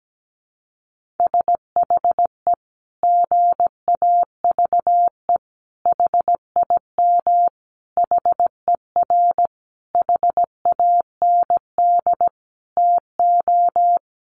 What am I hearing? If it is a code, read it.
Morse code, heard as a keyed tone, about 17 wpm, 700 Hz. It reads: SHE GAVE HIM HER HAND TO
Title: Sentences from Top 1400 Words 17wpm